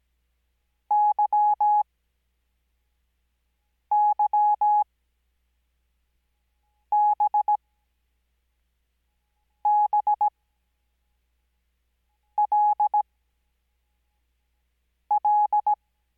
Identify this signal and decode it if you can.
Morse code, heard as a keyed tone, about 17 wpm, spaced as for 4 wpm, 840 Hz. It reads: YYBBLL
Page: Farnsworth Spacing: 15WPM Character speed Spaced at 5WPM.